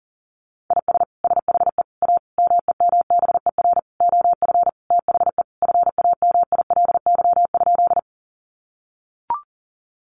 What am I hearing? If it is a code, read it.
Morse code, heard as a keyed tone, about 37 wpm, 700 Hz. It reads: IS SHE A MEMBER OF THE FAMILY?
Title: Sentences from Top 1100 Words 40wpm